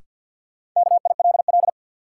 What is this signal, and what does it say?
Morse code, heard as a keyed tone, about 52 wpm, 700 Hz. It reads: KILL
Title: Sentences from Top 1400 Words - New Word Drill Alt - 50wpm